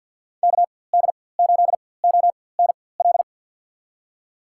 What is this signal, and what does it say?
Morse code, heard as a keyed tone, about 52 wpm, spaced as for 23 wpm, 700 Hz. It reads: KD8ONL
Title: USA Call Signs - 6 Characters - 4x Character Spacing - 50wpm